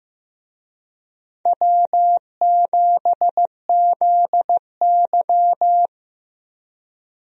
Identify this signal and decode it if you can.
Morse code, heard as a keyed tone, about 15 wpm, 700 Hz. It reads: W7ZY
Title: USA Call Signs - 4 Characters - Repeated 3x - 15wpm